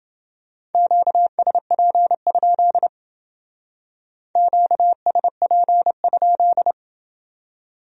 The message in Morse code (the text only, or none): QSP? QSP?